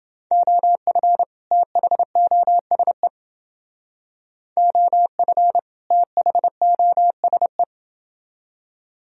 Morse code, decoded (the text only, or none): OF THOSE OF THOSE